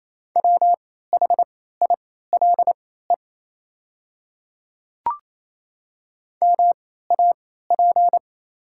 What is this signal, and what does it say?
Morse code, heard as a keyed tone, about 27 wpm, spaced as for 16 wpm, 700 Hz. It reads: WHILE MAP
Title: 3x Character Spacing - Top 500 Words - 28wpm